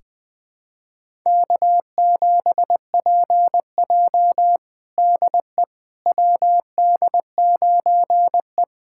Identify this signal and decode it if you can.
Morse code, heard as a keyed tone, about 20 wpm, 700 Hz. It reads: K7PJ DE WD9E
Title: USA Callsign DE Callsign - 4 Characters - 20wpm